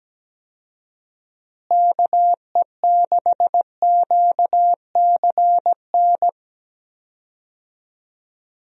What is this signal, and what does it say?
Morse code, heard as a keyed tone, about 17 wpm, 700 Hz. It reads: KE6QCN
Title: USA Call Signs - 6 Characters - Repeated 2x - 17wpm